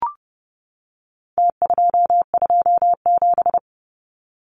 Morse code, heard as a keyed tone, about 30 wpm, 700 Hz. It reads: T227
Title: Numbers - Contest Style - 4-Digit 30wpm